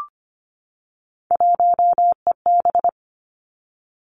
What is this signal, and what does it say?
Morse code, heard as a keyed tone, about 25 wpm, 700 Hz. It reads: 1EB
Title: Three Challenging Letter-Number Combinations 25wpm